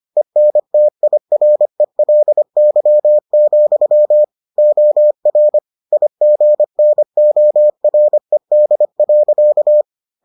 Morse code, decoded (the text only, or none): ENTIRELY, OR IGNORED.